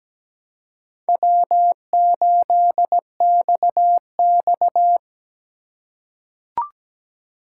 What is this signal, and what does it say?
Morse code, heard as a keyed tone, about 17 wpm, 700 Hz. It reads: W8XX
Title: USA Call Signs - 4 Characters - 4x Character Spacing - 17wpm